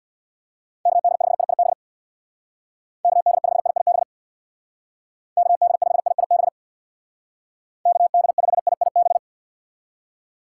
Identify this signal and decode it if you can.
Morse code, heard as a keyed tone, about 62 wpm, 690 Hz. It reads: KD5IIB KD5IIB KD5IIB KD5IIB
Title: Speed Racing - USA 2x3 Call Signs 50wpm